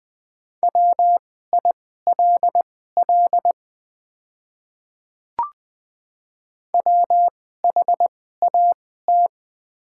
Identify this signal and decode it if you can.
Morse code, heard as a keyed tone, about 21 wpm, spaced as for 15 wpm, 700 Hz. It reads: WILL WHAT
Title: 2x Character Spacing - Top 500 Words - 20wpm